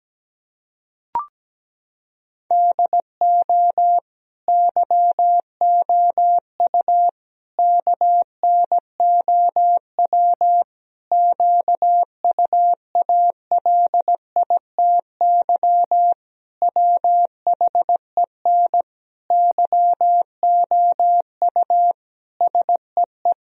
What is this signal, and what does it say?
Morse code, heard as a keyed tone, about 17 wpm, 700 Hz. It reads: DO YOU KNOW QUALITY WHEN YOU SEE
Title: Sentences from Top 1000 Words - Review Part 4 - 17wpm